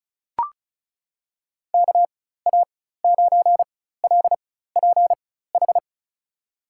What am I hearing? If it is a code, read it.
Morse code, heard as a keyed tone, about 36 wpm, spaced as for 16 wpm, 700 Hz. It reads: KA9LPH
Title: USA Call Signs - 4x Character Spacing - 35wpm